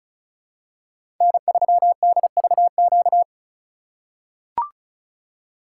Morse code, heard as a keyed tone, about 34 wpm, 700 Hz. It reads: N3DVQ
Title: USA Call Signs - 5 Characters - 2x Character Spacing - 35wpm